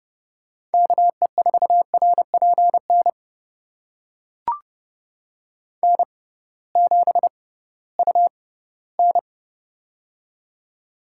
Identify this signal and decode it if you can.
Morse code, heard as a keyed tone, about 31 wpm, spaced as for 18 wpm, 700 Hz. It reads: KE4RPN N7UN
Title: USA Call Signs - 6x Character Spacing - 30wpm